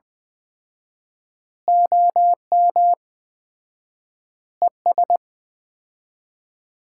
Morse code, heard as a keyed tone, about 20 wpm, 700 Hz. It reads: OM ES
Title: Common QSO Phrases - 4x Word-Spacing - 20wpm